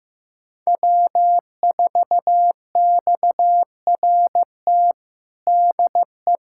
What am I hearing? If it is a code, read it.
Morse code, heard as a keyed tone, about 15 wpm, 700 Hz. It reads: W4XRT DE
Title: USA Callsign DE Callsign 15wpm